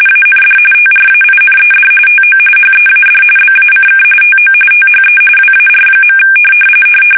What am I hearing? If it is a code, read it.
RTTY, about 45.45 baud. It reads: AND TRAINED TO HEAD BACK TO THE WAR.
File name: fsk850 / battleshipnj